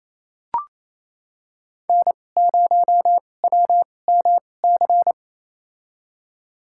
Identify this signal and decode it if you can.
Morse code, heard as a keyed tone, about 28 wpm, spaced as for 20 wpm, 700 Hz. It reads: N0WMC
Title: USA Call Signs - 5 Characters - 2x Character Spacing - 28wpm